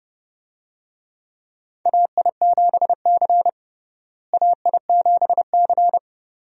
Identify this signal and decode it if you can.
Morse code, heard as a keyed tone, about 30 wpm, 700 Hz. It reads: AI7C AI7C
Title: USA Call Signs - 4 Characters - Repeated 2x - 30wpm